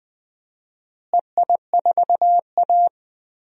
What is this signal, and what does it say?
Morse code, heard as a keyed tone, about 20 wpm, 700 Hz. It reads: EI4A
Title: DXCC-100 Call Signs - 4 Characters - Repeated 2x - 20wpm